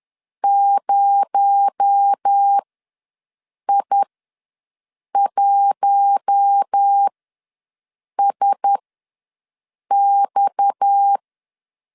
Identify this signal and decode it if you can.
Morse code, heard as a keyed tone, about 10 wpm, spaced as for 6 wpm, 800 Hz. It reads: 0I1SX
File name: LEZIONI / 6 PKX0 / 6 PKX0 ws 10wpm 4220